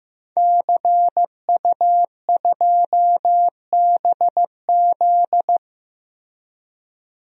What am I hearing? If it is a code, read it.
Morse code, heard as a keyed tone, about 15 wpm, 700 Hz. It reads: CU2BZ
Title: DXCC-100 Call Signs - 5 Characters - Repeated 2x - 15wpm